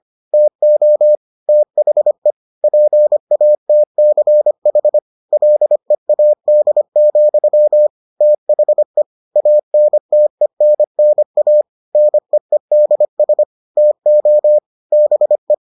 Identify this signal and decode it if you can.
Morse code, heard as a keyed tone, about 25 wpm, 600 Hz. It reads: TO THE PATCH LEAD, THE ANTENNA NEEDS TO BE